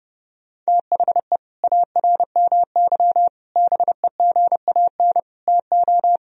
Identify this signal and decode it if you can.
Morse code, heard as a keyed tone, about 30 wpm, 700 Hz. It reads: THE ARMY BEGAN TO
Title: Sentences from Top 1000 Words - Review Part 3 - 30wpm